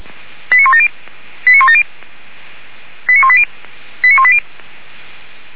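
Five-tone selective call (ZVEI), heard as quick runs of five tones, tones of about 70 ms. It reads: 89170 89170 89180 89180
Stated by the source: ZVEI five-tone call received on SDR, from audio.f85360.feuerwehr-orlamuende.org (fire brigade Orlamünde).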